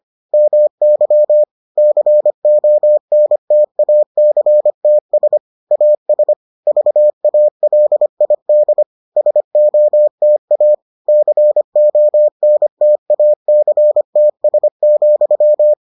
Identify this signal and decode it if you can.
Morse code, heard as a keyed tone, about 25 wpm, 600 Hz. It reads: MY CONTACTS AS VALID SOTA CONTACTS,